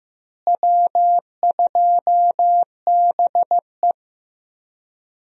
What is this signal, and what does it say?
Morse code, heard as a keyed tone, about 15 wpm, 700 Hz. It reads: W2BE
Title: USA Call Signs - 4 Characters - Repeated 2x - 15wpm